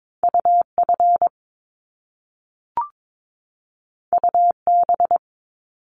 Morse code, heard as a keyed tone, about 22 wpm, 700 Hz. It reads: UF UB
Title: Top Two Letter Combinations in Words 22wpm